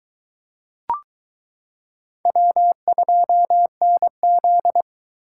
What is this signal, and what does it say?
Morse code, heard as a keyed tone, about 23 wpm, 700 Hz. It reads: W2NZ